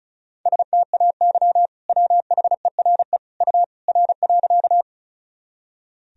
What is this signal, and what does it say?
Morse code, heard as a keyed tone, about 36 wpm, 700 Hz. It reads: STAY WHERE U R.
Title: Sentences from Top 500 Words and Abbreviations 35wpm